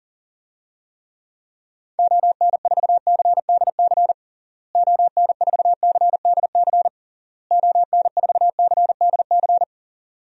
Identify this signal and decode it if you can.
Morse code, heard as a keyed tone, about 40 wpm, 700 Hz. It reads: ON4CDC ON4CDC ON4CDC
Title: DXCC-100 Call Signs - 6 Characters - Repeated 3x - 40wpm